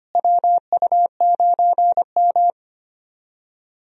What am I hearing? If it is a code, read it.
Morse code, heard as a keyed tone, about 25 wpm, 700 Hz. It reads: WU9M
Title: USA Call Signs - 4 Characters - Repeated 2x - 25wpm